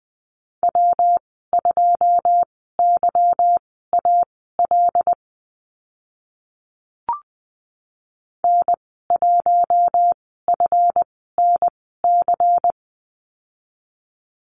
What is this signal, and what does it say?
Morse code, heard as a keyed tone, about 20 wpm, spaced as for 14 wpm, 700 Hz. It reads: W2YAL N1FNC
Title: 161 Lesson 26g - Letter g - Call Signs 20wpm